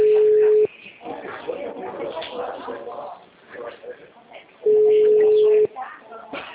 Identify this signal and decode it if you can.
Ringback tone (Europe, 425 Hz), with speech.
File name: out- -2016- - - .1437